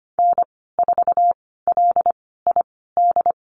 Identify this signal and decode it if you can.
Morse code, heard as a keyed tone, about 25 wpm, spaced as for 16 wpm, 690 Hz. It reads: N4LID
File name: Lesson 15g - Number 5 - Call Signs 25wpm